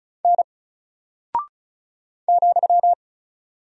Morse code, heard as a keyed tone, about 36 wpm, 700 Hz. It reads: N ,
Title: Single Character (Letter:Number:Punctuation:Prosign) Alt 35wpm